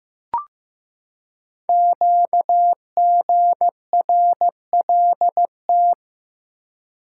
Morse code, heard as a keyed tone, about 15 wpm, 700 Hz. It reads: QGRLT
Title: Code Groups - 5 Letters - 15wpm